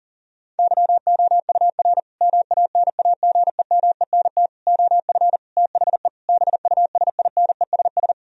Morse code, heard as a keyed tone, about 37 wpm, 700 Hz. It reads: YOUR MANAGEMENT OF THE BUSINESS
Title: Sentences from Top 1000 Words - Review Part 4 - 40wpm